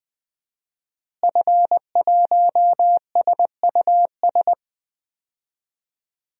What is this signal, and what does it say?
Morse code, heard as a keyed tone, about 20 wpm, 700 Hz. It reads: F1SUS
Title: DXCC-100 Call Signs - 5 Characters - Repeated 3x - 20wpm